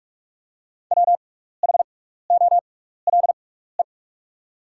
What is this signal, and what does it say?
Morse code, heard as a keyed tone, about 44 wpm, spaced as for 15 wpm, 700 Hz. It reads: WHOLE
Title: 6x Character Spacing - Top 500 Words - 45wpm